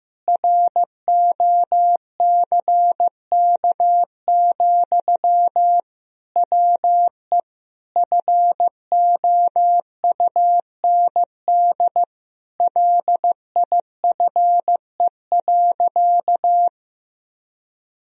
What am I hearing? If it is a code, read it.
Morse code, heard as a keyed tone, about 16 wpm, 700 Hz. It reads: ROCK, WE FOUND LIFE.
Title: Sentences from Top 1000 Words - Review Part 2 - 15wpm